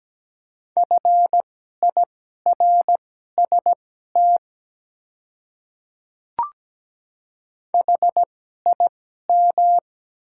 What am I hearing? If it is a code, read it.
Morse code, heard as a keyed tone, about 17 wpm, spaced as for 12 wpm, 700 Hz. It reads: FIRST HIM
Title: 2x Character Spacing - Top 100 Words - 17wpm